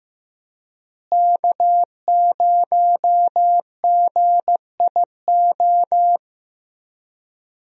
Morse code, heard as a keyed tone, about 15 wpm, 700 Hz. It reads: K0GIO
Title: USA Call Signs - 5 Characters - Repeated 3x - 15wpm